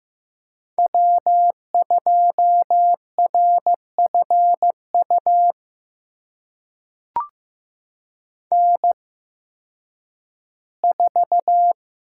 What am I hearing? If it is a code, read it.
Morse code, heard as a keyed tone, about 15 wpm, spaced as for 10 wpm, 700 Hz. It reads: W2RFU N4
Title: USA Call Signs - 8x Character Spacing - 15wpm